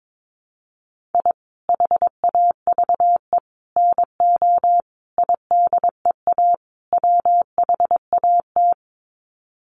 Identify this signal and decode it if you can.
Morse code, heard as a keyed tone, about 22 wpm, 700 Hz. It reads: I HAVE NO IDEA WHAT
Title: Sets of 5 Words Encore 22wpm